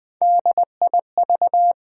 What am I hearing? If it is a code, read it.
Morse code, heard as a keyed tone, about 20 wpm, 700 Hz. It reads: DIV
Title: Build A Word - 1 Letter At A Time - 3 to 5-Letter Words - 20wpm